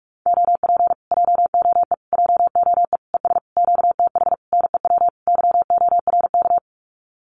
Morse code, heard as a keyed tone, about 46 wpm, 700 Hz. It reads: OP JOE JOE ES QTH NEW YORK
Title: Common QSO Phrases Encore 45wpm